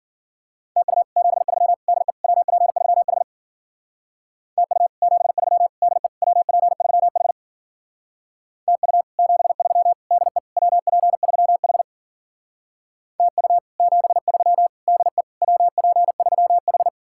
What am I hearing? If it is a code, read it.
Morse code, heard as a keyed tone, about 48 wpm, 700 Hz. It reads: TU 73 DE WP3H TU 73 DE WP3H TU 73 DE WP3H TU 73 DE WP3H
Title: Speed-Racing Common QSO Phrases - 40wpm